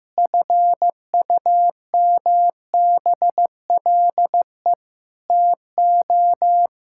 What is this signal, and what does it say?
Morse code, heard as a keyed tone, about 15 wpm, 700 Hz. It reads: FUMBLE TO
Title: Pangram Challenge v2 - 15wpm